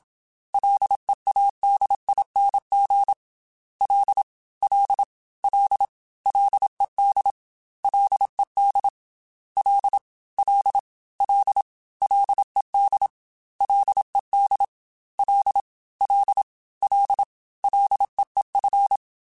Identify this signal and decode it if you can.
Morse code, heard as a keyed tone, about 27 wpm, 800 Hz. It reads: LEADING L L L LED LED L L L LED LED L L L LEEF
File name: Ls / Single L m3s / Single L @25